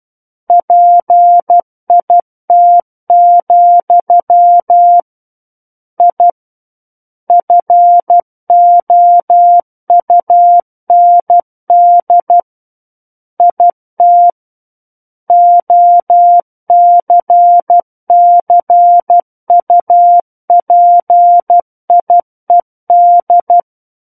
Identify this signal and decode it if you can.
Morse code, heard as a keyed tone, about 12 wpm, 700 Hz. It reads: PIT, I FOUND IT OCCUPIED
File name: Book One / 03-Chapter 3 - 12 WPM